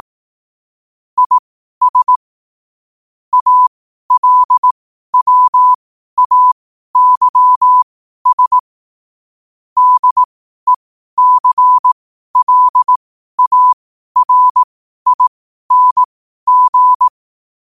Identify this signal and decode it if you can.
Morse code, heard as a keyed tone, about 18 wpm, spaced as for 12 wpm, 1000 Hz. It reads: IS ALWAYS DECLARING